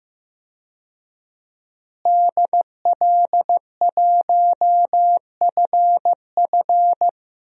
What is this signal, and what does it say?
Morse code, heard as a keyed tone, about 15 wpm, 700 Hz. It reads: DL1FF
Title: DXCC-100 Call Signs - 5 Characters - Repeated 2x - 15wpm